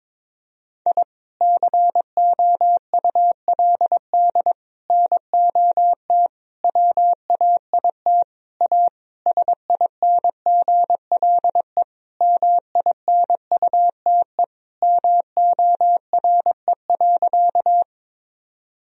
Morse code, heard as a keyed tone, about 22 wpm, 700 Hz. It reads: I COULD NOT WAIT A SINGLE MINUTE MORE.
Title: Sentences from Top 1000 Words - Review Part 4 - 22wpm